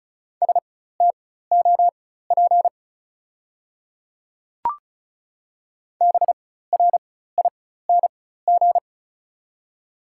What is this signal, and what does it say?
Morse code, heard as a keyed tone, about 34 wpm, spaced as for 16 wpm, 690 Hz. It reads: STOP BRING
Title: 4x Character Spacing - Top 500 Words - 35wpm